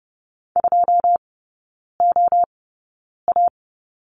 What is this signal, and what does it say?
Morse code, heard as a keyed tone, about 30 wpm, spaced as for 9 wpm, 700 Hz. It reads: 2OA
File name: 081 Lesson 14e - Number 2 - Three Characters 30wpm